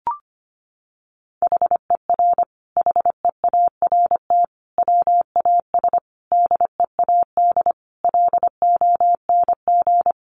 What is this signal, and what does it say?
Morse code, heard as a keyed tone, about 25 wpm, 700 Hz. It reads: HER HEART WAS DEAD LONG
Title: Sentences from Top 1500 Words 25wpm